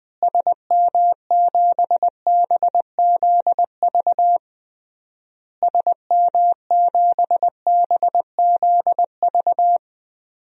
Morse code, heard as a keyed tone, about 20 wpm, 700 Hz. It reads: SM7BZV SM7BZV
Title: DXCC-100 Call Signs - 6 Characters - Repeated 3x - 20wpm